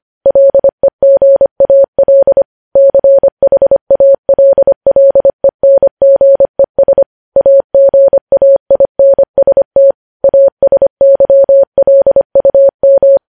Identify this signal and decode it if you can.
Morse code, heard as a keyed tone, about 25 wpm, 550 Hz. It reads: LEGAL CHALLENGES AGAINST ASYLUM
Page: News Headlines in Morse Code at 25 WPM